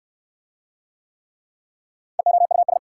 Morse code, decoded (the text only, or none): EOUS